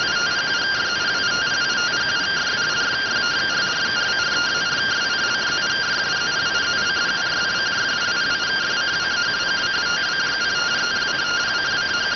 Two-tone teleprinter signal, about 170 Hz shift.